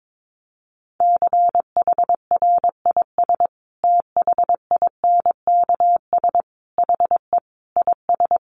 Morse code, heard as a keyed tone, about 22 wpm, 700 Hz. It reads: CHRIS THINKS HE IS
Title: Easy Phrases 22wpm